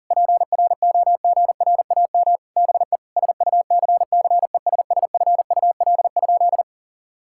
Morse code, heard as a keyed tone, about 37 wpm, 700 Hz. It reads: PROGRAM BE SUCCESSFUL?
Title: Sentences from Top 1000 Words - Review Part 5 - 40wpm